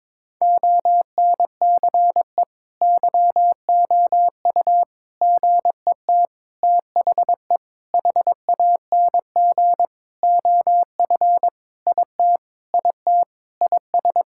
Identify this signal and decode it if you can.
Morse code, heard as a keyed tone, about 22 wpm, 700 Hz. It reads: ONCE YOU GET THE HANG OF IT IT IS
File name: Sentences with 7-Word Phrases - 22wpm